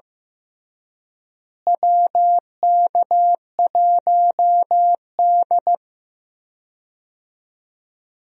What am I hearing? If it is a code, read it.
Morse code, heard as a keyed tone, about 15 wpm, 700 Hz. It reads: WK1D
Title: USA Call Signs - 4 Characters - Repeated 2x - 15wpm